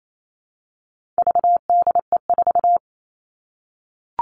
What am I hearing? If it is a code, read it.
Morse code, heard as a keyed tone, about 28 wpm, 700 Hz. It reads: VDE4
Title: All Cut Numbers - 4-Digit 28wpm